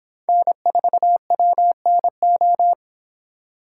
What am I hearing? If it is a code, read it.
Morse code, heard as a keyed tone, about 26 wpm, 700 Hz. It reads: N4WNO
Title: Speed Racing - USA 1x3 Call Signs 22wpm